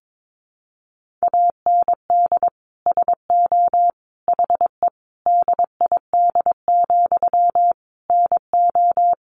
Morse code, heard as a keyed tone, about 22 wpm, 700 Hz. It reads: AND SO HE DID, NO
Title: Sentences from Top 1500 Words 22wpm